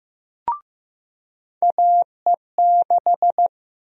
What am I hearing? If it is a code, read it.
Morse code, heard as a keyed tone, about 16 wpm, 700 Hz. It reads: AE6
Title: USA Call Sign Prefixes - Repeated 2x - 15wpm